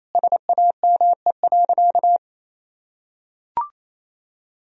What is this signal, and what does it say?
Morse code, heard as a keyed tone, about 28 wpm, 700 Hz. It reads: SAME.
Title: Sentences from Top 1000 Words - Review Part 3 - 28wpm